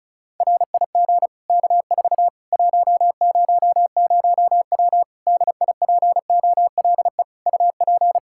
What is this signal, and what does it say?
Morse code, heard as a keyed tone, about 36 wpm, 700 Hz. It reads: RIG K4 100W DIPOLE UP
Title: Practice QSO - Exchange 2 - Info - 35wpm